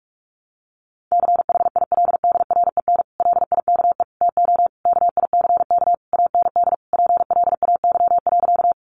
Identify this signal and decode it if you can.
Morse code, heard as a keyed tone, about 46 wpm, 700 Hz. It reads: CHILDREN LIKE TO KICK AND PLAY.
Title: Sentences from Top 1300 Words 45wpm